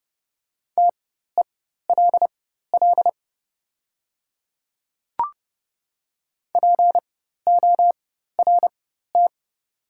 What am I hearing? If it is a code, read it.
Morse code, heard as a keyed tone, about 30 wpm, spaced as for 14 wpm, 700 Hz. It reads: TELL PORT
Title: 4x Character Spacing - Top 200 Words - 30wpm